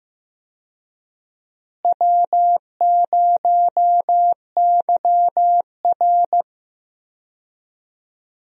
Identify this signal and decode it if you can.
Morse code, heard as a keyed tone, about 15 wpm, 700 Hz. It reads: W0YR
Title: USA Call Signs - 4 Characters - Repeated 2x - 15wpm